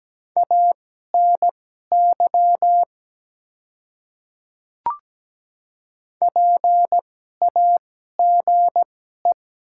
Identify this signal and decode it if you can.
Morse code, heard as a keyed tone, about 17 wpm, spaced as for 12 wpm, 690 Hz. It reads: ANY PAGE
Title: 2x Character Spacing - Top 300 Words - 17wpm